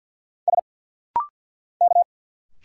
Morse code, heard as a keyed tone, about 47 wpm, 700 Hz. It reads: S K
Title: Single Letters - Rapid-Fire 50wpm